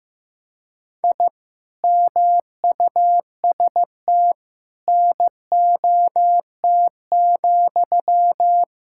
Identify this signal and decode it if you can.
Morse code, heard as a keyed tone, about 15 wpm, 700 Hz. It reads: I MUST NOT,
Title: Sentences from Top 1000 Words - Review Part 5 - 15wpm